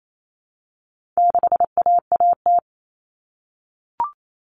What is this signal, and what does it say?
Morse code, heard as a keyed tone, about 28 wpm, 700 Hz. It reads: 6AAT
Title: Common Cut Numbers - 4-Digit 28wpm